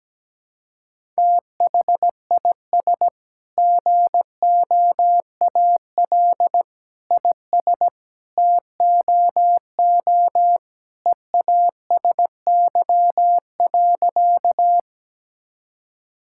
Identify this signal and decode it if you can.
Morse code, heard as a keyed tone, about 17 wpm, 700 Hz. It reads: THIS GOAL IS TOO EASY.
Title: Sentences from Top 1000 Words - Review Part 5 - 17wpm